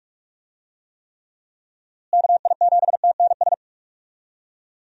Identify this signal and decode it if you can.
Morse code, heard as a keyed tone, about 43 wpm, 700 Hz. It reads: KI7TNS
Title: USA Call Signs - 6 Characters - Repeated 3x - 45wpm